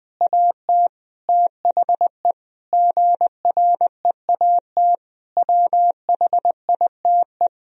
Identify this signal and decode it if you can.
Morse code, heard as a keyed tone, about 20 wpm, 700 Hz. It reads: AT THE GREAT WHITE
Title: Sentences from Top 1000 Words - Review Part 1 - 20wpm